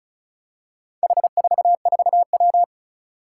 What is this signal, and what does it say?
Morse code, heard as a keyed tone, about 36 wpm, 700 Hz. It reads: H44W
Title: Four Challenging Letter-Number Combinations 35wpm